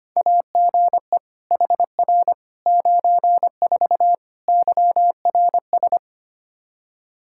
Morse code, heard as a keyed tone, about 25 wpm, 700 Hz. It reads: AGE HR 94 YRS
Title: K1USN SST QSO Practice 25wpm